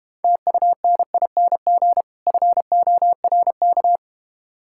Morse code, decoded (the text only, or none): TUNING FORK